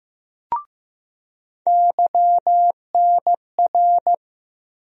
Morse code, heard as a keyed tone, about 16 wpm, 700 Hz. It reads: YNR
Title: USA Call Sign Postfixes - 3 Letters 15wpm